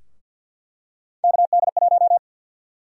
Morse code, heard as a keyed tone, about 52 wpm, 700 Hz. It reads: KD1
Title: USA Call Sign Prefixes - 50wpm